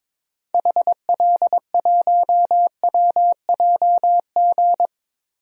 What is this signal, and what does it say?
Morse code, heard as a keyed tone, about 22 wpm, 700 Hz. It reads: HL1WJG
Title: DXCC-100 Call Signs - 6 Characters - Repeated 2x - 22wpm